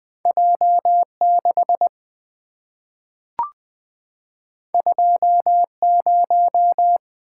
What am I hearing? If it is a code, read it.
Morse code, heard as a keyed tone, about 20 wpm, 700 Hz. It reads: J6 20
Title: Two Letter-Number Combinations - 20wpm